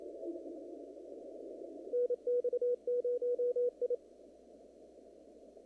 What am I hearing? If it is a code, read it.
Morse code, heard as a keyed tone, about 29 wpm, 490 Hz. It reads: NX0I
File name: CW / kt5j wpx cw-op-k5pi